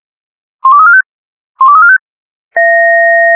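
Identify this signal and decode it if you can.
Five-tone selective call (ZVEI), heard as quick runs of five tones, tones of about 70 ms. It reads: 12345 12345
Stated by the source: Five-tone ZVEI selective call received on an SDR.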